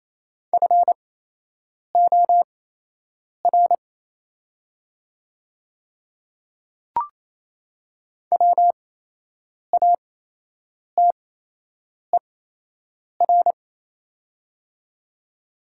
Morse code, heard as a keyed tone, about 28 wpm, spaced as for 8 wpm, 700 Hz. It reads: FOR WATER
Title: 8x Character Spacing - Top 200 Words - 28wpm